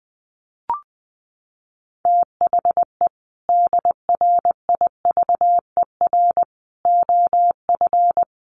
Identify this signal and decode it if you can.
Morse code, heard as a keyed tone, about 20 wpm, 700 Hz. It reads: THE DRIVER OF
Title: Sentences from Top 1000 Words 20wpm